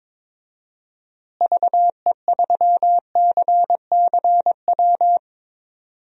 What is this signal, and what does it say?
Morse code, heard as a keyed tone, about 22 wpm, 700 Hz. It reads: VE3CCW
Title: DXCC-100 Call Signs - 6 Characters - Repeated 3x - 22wpm